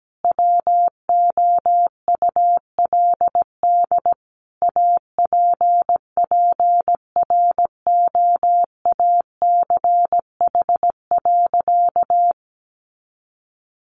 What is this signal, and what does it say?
Morse code, heard as a keyed tone, about 17 wpm, 700 Hz. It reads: WOULD APPROACH.
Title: Sentences from Top 1400 Words 17wpm